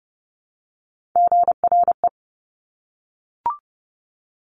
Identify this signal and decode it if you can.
Morse code, heard as a keyed tone, about 30 wpm, 700 Hz. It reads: GRE
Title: Top Three Letter Combinations in Words 30wpm